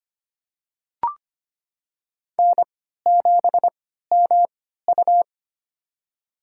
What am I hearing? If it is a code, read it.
Morse code, heard as a keyed tone, about 25 wpm, spaced as for 14 wpm, 700 Hz. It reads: N7MU
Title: USA Call Signs - 3x Character Spacing - 25wpm